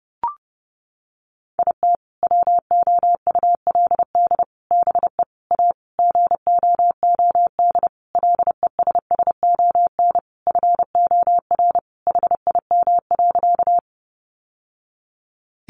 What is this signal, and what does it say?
Morse code, heard as a keyed tone, about 30 wpm, 700 Hz. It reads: IT WOULD BE A GOOD LESSON FOR HIM.
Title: Sentences from Top 1200 Words 30wpm